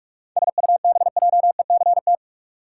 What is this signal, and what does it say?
Morse code, heard as a keyed tone, about 46 wpm, 700 Hz. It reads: SUBJECT
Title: Sentences from Top 800 Words - New Word Drill - 45wpm